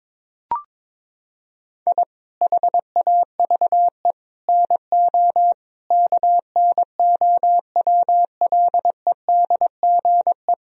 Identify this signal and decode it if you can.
Morse code, heard as a keyed tone, about 22 wpm, 700 Hz. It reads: I HAVE NO KNOWLEDGE
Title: Sentences from Top 1000 Words - Review Part 3 - 22wpm